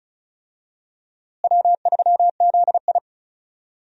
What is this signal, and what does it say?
Morse code, heard as a keyed tone, about 34 wpm, 700 Hz. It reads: W3ZI
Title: USA Call Signs - 4 Characters - Repeated 2x - 35wpm